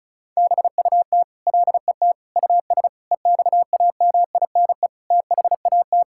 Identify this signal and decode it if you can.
Morse code, heard as a keyed tone, about 36 wpm, 700 Hz. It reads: BUT LET US EXAMINE THAT
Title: Sentences from Top 1600 Words 35wpm